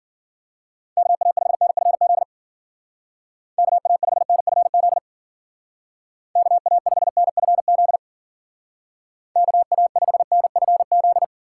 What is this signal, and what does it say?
Morse code, heard as a keyed tone, about 51 wpm, 700 Hz. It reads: KA5NFZ KA5NFZ KA5NFZ KA5NFZ